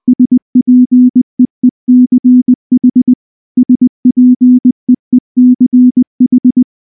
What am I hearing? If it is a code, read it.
Morse code, heard as a keyed tone, about 20 wpm, 250 Hz. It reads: SPEECH SPEECH
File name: morse / morse-training-20WPM-6